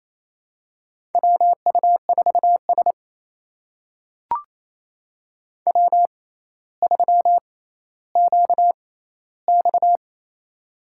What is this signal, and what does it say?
Morse code, heard as a keyed tone, about 28 wpm, spaced as for 14 wpm, 700 Hz. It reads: WU4S W3QX
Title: USA Call Signs - 6x Character Spacing - 28wpm